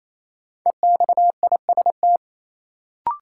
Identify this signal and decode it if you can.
Morse code, heard as a keyed tone, about 28 wpm, 700 Hz. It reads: EXIST
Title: Sentences from Top 1100 Words - New Word Drill Alt - 28wpm